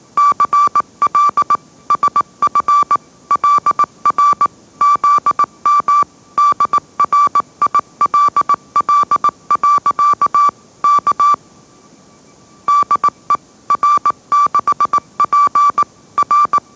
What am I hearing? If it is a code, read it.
Morse code, heard as a keyed tone, about 18 wpm, 1160 Hz. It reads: CL SF LR ZM DRILL. K DE R6P R
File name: mrx / mrx081 20wpm